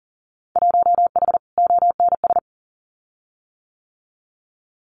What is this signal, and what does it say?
Morse code, heard as a keyed tone, about 40 wpm, 700 Hz. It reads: 1H ONS
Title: Winter Field Day Exchanges 40wpm